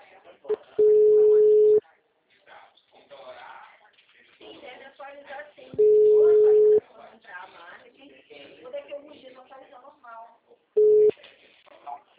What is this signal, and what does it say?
Ringback tone (Europe, 425 Hz), with speech.